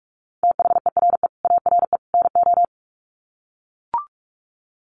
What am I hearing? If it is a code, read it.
Morse code, heard as a keyed tone, about 44 wpm, 700 Hz. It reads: THERE ARE NO
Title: Sets of 3 Words 45wpm